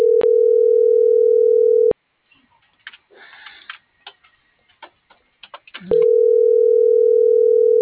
Ringback tone (North America), 440 + 480 Hz.
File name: CENTRAL MABE / out celeste.alcantara - .49048